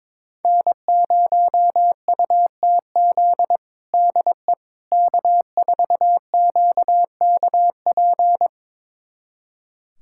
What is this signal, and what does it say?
Morse code, heard as a keyed tone, about 22 wpm, 700 Hz. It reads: N0UTZ DE K4QKP
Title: USA Callsign DE Callsign - 5 Characters - 22wpm